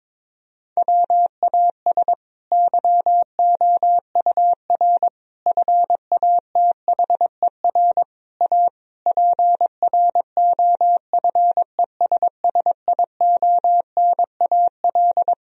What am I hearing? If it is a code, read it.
Morse code, heard as a keyed tone, about 22 wpm, 700 Hz. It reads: WAS YOUR FATHER A PROFESSIONAL
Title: Sentences from Top 1000 Words - Review Part 4 - 22wpm